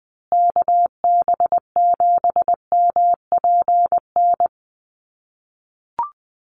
Morse code, heard as a keyed tone, about 20 wpm, 700 Hz. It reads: KB7MPN
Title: USA Call Signs Repeated 3x 20wpm